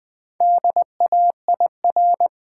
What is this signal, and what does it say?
Morse code, heard as a keyed tone, about 20 wpm, 700 Hz. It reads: DAIR
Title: Build A Word - 1 Letter At A Time - 3 to 5-Letter Words - 20wpm